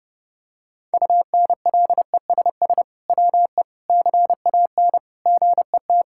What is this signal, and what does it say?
Morse code, heard as a keyed tone, about 30 wpm, 700 Hz. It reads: UNLESS WE CAN GET
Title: Sentences from Top 1000 Words - Review Part 6 - 30wpm